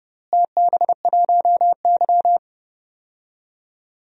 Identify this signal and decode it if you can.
Morse code, heard as a keyed tone, about 31 wpm, 700 Hz. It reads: TB1Y
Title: DXCC-100 Call Signs - 4 Characters - Repeated 3x - 30wpm